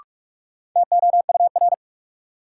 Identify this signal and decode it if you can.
Morse code, heard as a keyed tone, about 46 wpm, 700 Hz. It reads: TOUR
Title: Sentences from Top 1400 Words - New Word Drill - 45wpm